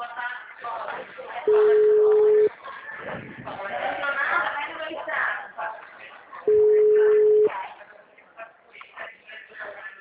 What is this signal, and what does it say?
Ringback tone (Europe, 425 Hz), with speech.